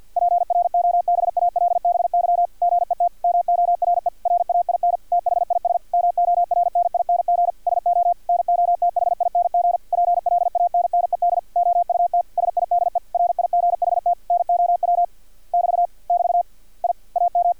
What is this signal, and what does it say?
Morse code, heard as a keyed tone, about 50 wpm, 700 Hz. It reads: PROBALBY GET MORE RAIN THIS MORNING SO NOTHING PLANNED OUT SIDE RIGHT NOW <BT> <BT> I AM
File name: archived / CRSnet